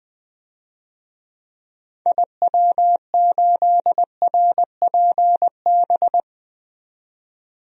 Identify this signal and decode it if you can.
Morse code, heard as a keyed tone, about 21 wpm, 700 Hz. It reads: IW8RPB
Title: DXCC-100 Call Signs - 6 Characters - 20wpm